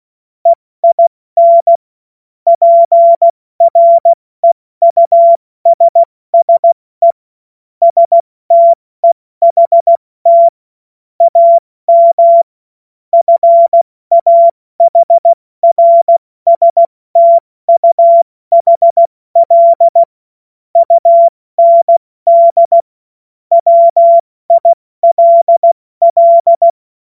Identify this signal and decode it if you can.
Morse code, heard as a keyed tone, about 16 wpm, 680 Hz. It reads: EIN PREUSSE STEHT AM FAHRSTUHL UND WILL
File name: Witz-26-August-2025-70-F80